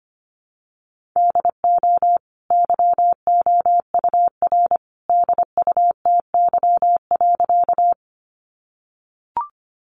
Morse code, heard as a keyed tone, about 25 wpm, 700 Hz. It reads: DO YOUR DUTY.
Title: Sentences from Top 1500 Words 25wpm